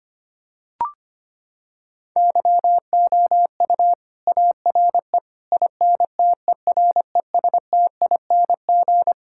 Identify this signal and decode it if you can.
Morse code, heard as a keyed tone, about 25 wpm, 700 Hz. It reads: YOU ARE INTERESTING
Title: Sentences from Top 1000 Words - Review Part 6 - 25wpm